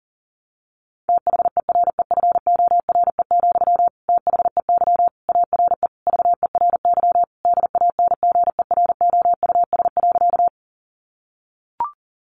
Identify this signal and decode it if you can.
Morse code, heard as a keyed tone, about 40 wpm, 700 Hz. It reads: THEREFORE, THEY ARE VERY DANGEROUS.
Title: Sentences from Top 1200 Words 40wpm